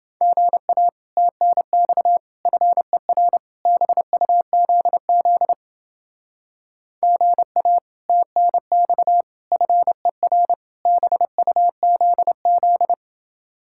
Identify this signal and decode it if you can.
Morse code, heard as a keyed tone, about 31 wpm, 700 Hz. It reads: GA TNX FER BUZZ GA TNX FER BUZZ